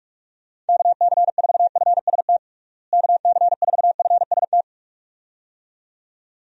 Morse code, heard as a keyed tone, about 44 wpm, 700 Hz. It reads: KC4FST KC4FST
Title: USA Call Signs - 6 Characters - Repeated 2x - 45wpm